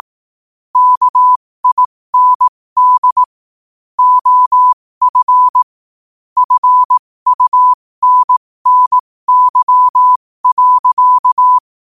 Morse code, heard as a keyed tone, about 18 wpm, spaced as for 15 wpm, 1000 Hz. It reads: KIND OF FUNNY.
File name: qotdfiles / 15wpm